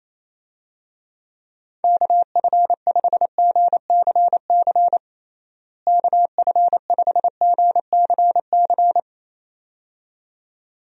Morse code, heard as a keyed tone, about 27 wpm, 700 Hz. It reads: KF5GCC KF5GCC
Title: USA Call Signs - 6 Characters - Repeated 2x - 28wpm